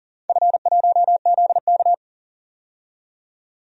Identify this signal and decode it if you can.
Morse code, heard as a keyed tone, about 40 wpm, 690 Hz. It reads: F1ZK
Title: DXCC-100 Call Signs - 4 Characters - 40wpm